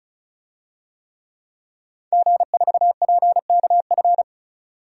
Morse code, heard as a keyed tone, about 36 wpm, 700 Hz. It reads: G4PKF
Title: DXCC-100 Call Signs - 5 Characters - 35wpm